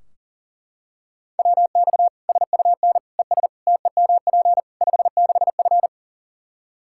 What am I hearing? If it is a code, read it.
Morse code, heard as a keyed tone, about 40 wpm, 700 Hz. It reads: WX SUN ES TEMP 56F
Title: Practice QSO - Exchange 2 - Info - 40wpm